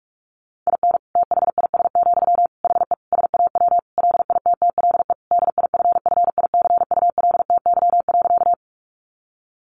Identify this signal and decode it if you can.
Morse code, heard as a keyed tone, about 44 wpm, 700 Hz. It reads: IN THIS, HE SAW LITTLE DIFFICULTY.
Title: Sentences from Top 1500 Words 45wpm